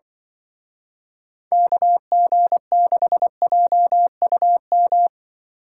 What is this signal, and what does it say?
Morse code, heard as a keyed tone, about 24 wpm, 700 Hz. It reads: KG6JUM
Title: Speed Racing - USA 2x3 Call Signs 20wpm